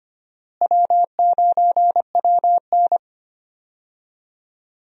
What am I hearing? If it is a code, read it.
Morse code, heard as a keyed tone, about 25 wpm, 700 Hz. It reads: W9WN
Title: USA Call Signs - 4 Characters - Repeated 3x - 25wpm